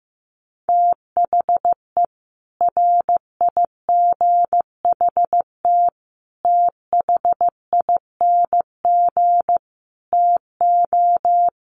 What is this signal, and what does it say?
Morse code, heard as a keyed tone, about 15 wpm, 700 Hz. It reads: THE RIGHT THING TO
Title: Sets of 5 Words 15wpm